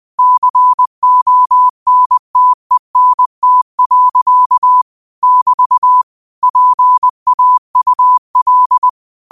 Morse code, heard as a keyed tone, about 20 wpm, 1000 Hz. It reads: CONTENT. <BT> PAUL
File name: qotdfiles / 20wpm